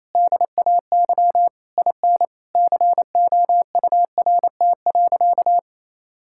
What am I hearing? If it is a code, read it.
Morse code, heard as a keyed tone, about 28 wpm, 700 Hz. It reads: DAY IN COURT.
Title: Sentences from Top 1600 Words 28wpm